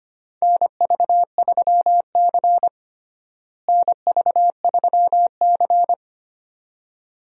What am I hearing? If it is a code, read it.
Morse code, heard as a keyed tone, about 25 wpm, 700 Hz. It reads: NV3C NV3C
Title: USA Call Signs - 4 Characters - Repeated 2x - 25wpm